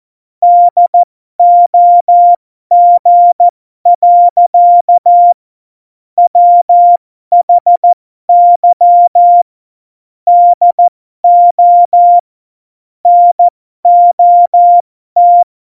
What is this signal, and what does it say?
Morse code, heard as a keyed tone, about 14 wpm, spaced as for 12 wpm, 700 Hz. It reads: DOG. WHY DO NOT
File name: Morsekurs / Witze